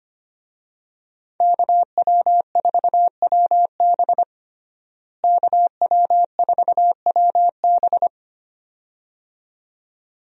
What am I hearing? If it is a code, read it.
Morse code, heard as a keyed tone, about 25 wpm, 700 Hz. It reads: KW4WB KW4WB
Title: USA Call Signs - 5 Characters - Repeated 2x - 25wpm